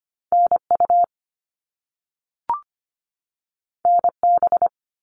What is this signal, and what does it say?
Morse code, heard as a keyed tone, about 25 wpm, 700 Hz. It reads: NU NB
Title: Canadian Provinces and Territories 25wpm